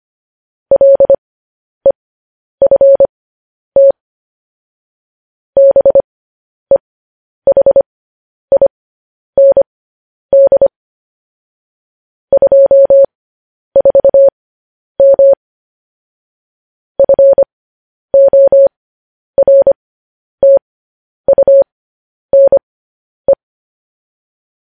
Morse code, heard as a keyed tone, about 25 wpm, spaced as for 10 wpm, 550 Hz. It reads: LEFT BEHIND 24M FORTUNE
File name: podcast / News headlines-1x-25x10